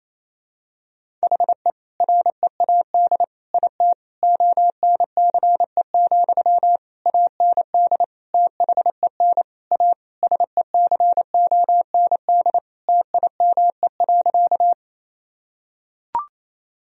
Morse code, heard as a keyed tone, about 28 wpm, 700 Hz. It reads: HE READ IT ONCE, AND THEN A SECOND TIME.
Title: Sentences from Top 1000 Words - Review Part 1 - 28wpm